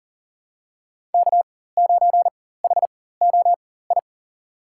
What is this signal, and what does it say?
Morse code, heard as a keyed tone, about 40 wpm, spaced as for 19 wpm, 700 Hz. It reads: K9HOI
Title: USA Call Signs - 5 Characters - 4x Character Spacing - 40wpm